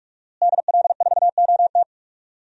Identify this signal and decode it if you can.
Morse code, heard as a keyed tone, about 45 wpm, 700 Hz. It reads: DL4OT
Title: USA Call Signs - 5 Characters - Repeated 3x - 45wpm